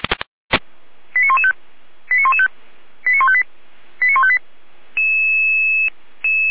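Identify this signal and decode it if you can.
Five-tone selective call (ZVEI), heard as quick runs of five tones, tones of about 70 ms. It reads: 89115 89115 89158 89158
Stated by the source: ZVEI five-tone selective call received on SDR.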